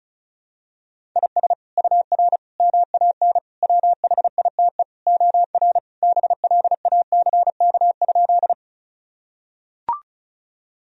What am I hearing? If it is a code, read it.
Morse code, heard as a keyed tone, about 36 wpm, 700 Hz. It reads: IS UR MAN WHITE OR BLACK?
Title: Sentences from Top 400 Words and Abbreviations 35wpm